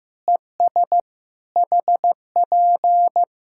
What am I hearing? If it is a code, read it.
Morse code, heard as a keyed tone, about 15 wpm, 700 Hz. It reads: ES HP
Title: Practice QSO - Exchange 4 - Ending - 15wpm